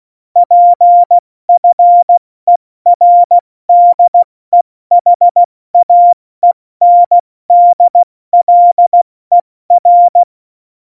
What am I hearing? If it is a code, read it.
Morse code, heard as a keyed tone, about 16 wpm, 700 Hz. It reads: PFERDEHAENDLER
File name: Morsekurs / Witze / Witz 26 Mai, 2020 70 F80